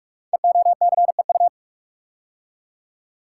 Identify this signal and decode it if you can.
Morse code, heard as a keyed tone, about 44 wpm, 690 Hz. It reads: EOCEU
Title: Code Groups - 5 Letter:Numbers - 45wpm